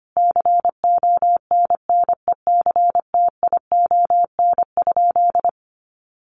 Morse code, heard as a keyed tone, about 25 wpm, 700 Hz. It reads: CONNECTION?
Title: Sentences from Top 1300 Words 25wpm